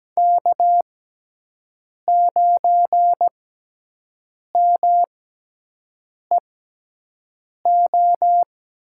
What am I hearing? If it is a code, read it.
Morse code, heard as a keyed tone, about 17 wpm, spaced as for 6 wpm, 700 Hz. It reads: K9MEO
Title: USA Call Signs - 5 Characters - 6x Character Spacing - 17wpm